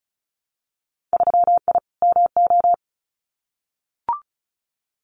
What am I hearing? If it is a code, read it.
Morse code, heard as a keyed tone, about 34 wpm, 700 Hz. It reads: 3I MO
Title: Winter Field Day Exchanges 35wpm